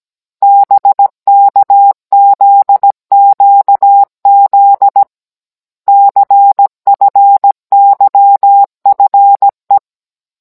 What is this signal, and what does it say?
Morse code, heard as a keyed tone, about 17 wpm, 800 Hz. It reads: BKZQZ CFYFE
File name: Cours-NR19-16mm